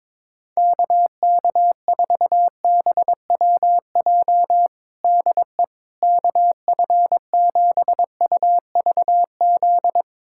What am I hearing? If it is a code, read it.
Morse code, heard as a keyed tone, about 22 wpm, 700 Hz. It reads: KK4BWJ DE KF7UVZ
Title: USA Callsign DE Callsign - 6 Characters - 22wpm